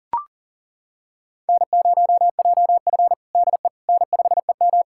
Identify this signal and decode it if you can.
Morse code, heard as a keyed tone, about 40 wpm, 700 Hz. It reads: N0JF DE N5EM
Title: USA Callsign DE Callsign - 4 Characters - 40wpm